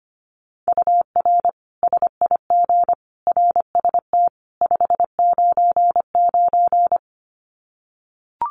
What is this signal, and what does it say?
Morse code, heard as a keyed tone, about 25 wpm, 700 Hz. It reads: UR SIG RST 599
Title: Common QSO Phrases 25wpm